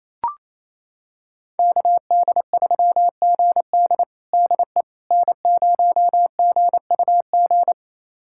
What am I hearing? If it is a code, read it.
Morse code, heard as a keyed tone, about 28 wpm, 700 Hz. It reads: KD3GD DE N0GUG
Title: USA Callsign DE Callsign - 5 Characters - 28wpm